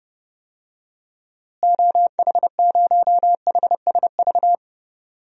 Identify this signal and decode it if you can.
Morse code, heard as a keyed tone, about 30 wpm, 700 Hz. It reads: OH0HSV
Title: DXCC-100 Call Signs - 6 Characters - Repeated 2x - 30wpm